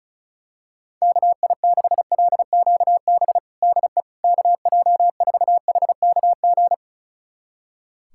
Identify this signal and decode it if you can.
Morse code, heard as a keyed tone, about 34 wpm, 700 Hz. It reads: KI6LQB DE KJ4HKG
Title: USA Callsign DE Callsign - 6 Characters - 35wpm